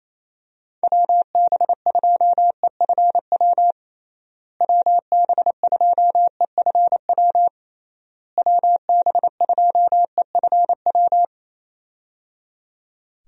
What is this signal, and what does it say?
Morse code, heard as a keyed tone, about 28 wpm, 700 Hz. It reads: WB2EFW WB2EFW WB2EFW
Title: USA Call Signs - 6 Characters - Repeated 3x - 28wpm